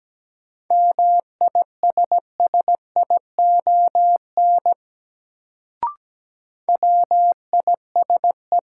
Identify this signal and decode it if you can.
Morse code, heard as a keyed tone, about 17 wpm, 700 Hz. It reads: MISSION WISE
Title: Sentences from Top 1600 Words - New Word Drill - 17wpm